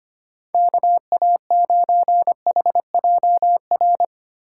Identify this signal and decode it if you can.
Morse code, heard as a keyed tone, about 25 wpm, 700 Hz. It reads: KA9HJR
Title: USA Call Signs - 6 Characters - 2x Character Spacing - 25wpm